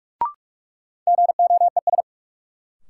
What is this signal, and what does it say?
Morse code, heard as a keyed tone, about 45 wpm, 700 Hz. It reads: GOES
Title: Sentences from Top 1500 Words - New Word Drill Alt - 45wpm